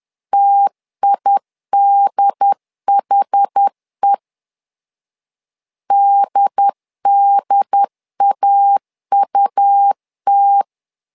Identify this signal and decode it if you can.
Morse code, heard as a keyed tone, about 10 wpm, 800 Hz. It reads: TIDHE DDAUT
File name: LEZIONI / 2 UHDN / 2 UHD 10wpm 1826